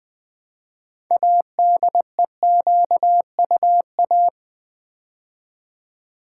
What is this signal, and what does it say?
Morse code, heard as a keyed tone, about 20 wpm, 700 Hz. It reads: ADEQUA
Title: Build A Word - 2 Letters At A Time - 8-Letter Words - 20wpm